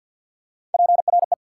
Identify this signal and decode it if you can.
Morse code, heard as a keyed tone, about 54 wpm, 700 Hz. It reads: PRE
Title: Prefixes and Suffixes 50wpm